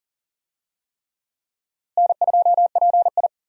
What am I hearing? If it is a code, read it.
Morse code, heard as a keyed tone, about 40 wpm, 700 Hz. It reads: N2PI
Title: USA Call Signs - 4 Characters - Repeated 3x - 40wpm